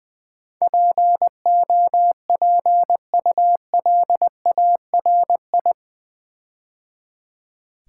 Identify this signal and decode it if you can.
Morse code, heard as a keyed tone, about 20 wpm, 700 Hz. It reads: POPULARI
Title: Build A Word - 4 Letters At A Time - 10-Letter Words - 20wpm